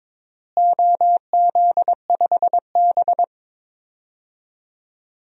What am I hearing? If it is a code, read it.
Morse code, heard as a keyed tone, about 21 wpm, 700 Hz. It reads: OZ5B
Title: DXCC-100 Call Signs - 4 Characters - Repeated 3x - 22wpm